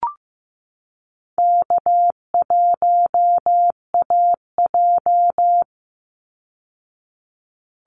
Morse code, heard as a keyed tone, about 15 wpm, 700 Hz. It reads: K1AJ
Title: Call Signs Repeated 3x - 15wpm